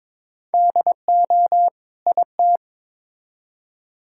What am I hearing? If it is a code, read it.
Morse code, heard as a keyed tone, about 22 wpm, 700 Hz. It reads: DO IT
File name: Build A Phrase - 2-Word Phrases - 22wpm